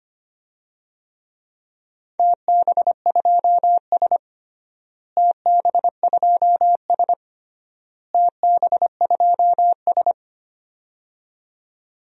Morse code, heard as a keyed tone, about 25 wpm, 700 Hz. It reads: TB2S TB2S TB2S
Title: DXCC-100 Call Signs - 4 Characters - Repeated 3x - 25wpm